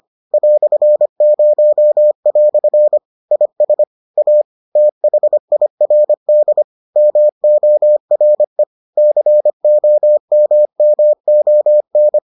Morse code, heard as a keyed tone, about 25 wpm, 600 Hz. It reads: "0" IS A THIRD MORE COMMON